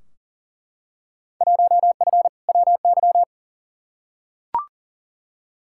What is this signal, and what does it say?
Morse code, heard as a keyed tone, about 40 wpm, 700 Hz. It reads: 1F WY
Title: Field Day Exchanges Encore - 40wpm